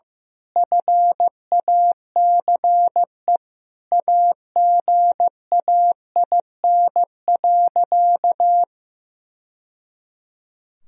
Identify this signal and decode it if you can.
Morse code, heard as a keyed tone, about 15 wpm, 700 Hz. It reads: FACE AGAIN.
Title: Sentences from Top 1000 Words - Review Part 2 - 15wpm